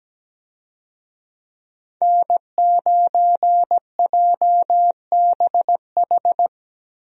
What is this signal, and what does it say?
Morse code, heard as a keyed tone, about 17 wpm, 700 Hz. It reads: N9JBH
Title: USA Call Signs - 5 Characters - Repeated 2x - 17wpm